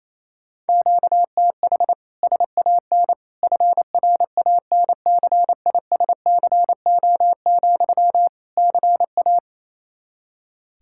Morse code, heard as a keyed tone, about 28 wpm, 700 Hz. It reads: QTH SAN FRANCISCO, CA
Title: Practice QSO - Exchange 1 - Beginning - 28wpm